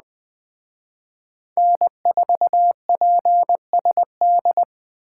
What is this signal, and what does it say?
Morse code, heard as a keyed tone, about 20 wpm, 700 Hz. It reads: N4PSD
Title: Speed Racing - USA 1x3 Call Signs 17wpm